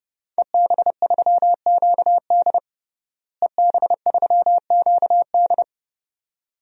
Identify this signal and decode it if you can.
Morse code, heard as a keyed tone, about 30 wpm, 700 Hz. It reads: EB3QD EB3QD
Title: DXCC-100 Call Signs - 5 Characters - Repeated 2x - 30wpm